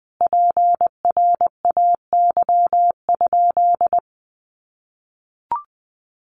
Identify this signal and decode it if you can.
Morse code, heard as a keyed tone, about 20 wpm, 700 Hz. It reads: PRAY?
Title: A Sound Of Thunder 20wpm